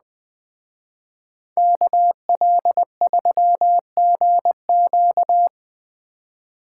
Morse code, heard as a keyed tone, about 19 wpm, 700 Hz. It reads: KL3GQ
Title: Speed Racing - USA 2x2 Call Signs 17wpm